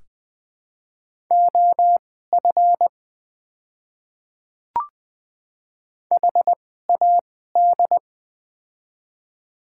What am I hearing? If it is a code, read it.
Morse code, heard as a keyed tone, about 20 wpm, spaced as for 14 wpm, 700 Hz. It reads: OF HAD
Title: 2x Character Spacing - Top 100 Words - 20wpm